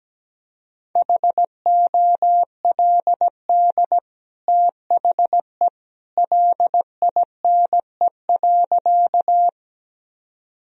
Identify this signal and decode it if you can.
Morse code, heard as a keyed tone, about 17 wpm, 700 Hz. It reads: HOLD THE LINE.
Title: Sentences from Top 1000 Words - Review Part 2 - 17wpm